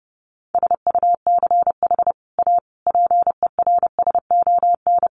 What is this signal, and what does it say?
Morse code, heard as a keyed tone, about 30 wpm, 700 Hz. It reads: SUCH A PERSON
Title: Sentences from Top 1400 Words 30wpm